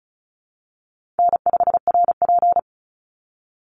Morse code, heard as a keyed tone, about 34 wpm, 700 Hz. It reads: N5RP
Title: Easy Call Signs Repeated 2x - 35wpm